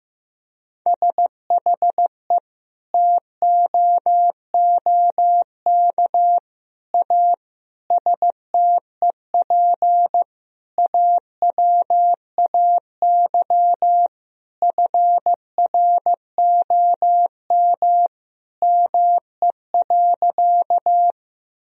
Morse code, heard as a keyed tone, about 16 wpm, 700 Hz. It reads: SHE TOOK A STEP AWAY FROM ME.
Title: Sentences from Top 1000 Words - Review Part 2 - 15wpm